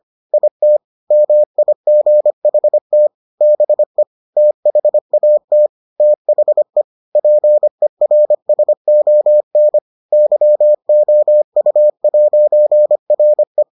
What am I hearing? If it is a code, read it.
Morse code, heard as a keyed tone, about 25 wpm, 600 Hz. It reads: IT MIGHT BE THAT THE PERSON YOU'RE